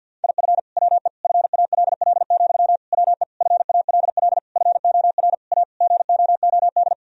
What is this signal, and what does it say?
Morse code, heard as a keyed tone, about 50 wpm, 700 Hz. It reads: IF WE FALL, WE FALL FOR A GOOD
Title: Sentences from Top 1000 Words - Review Part 2 - 50wpm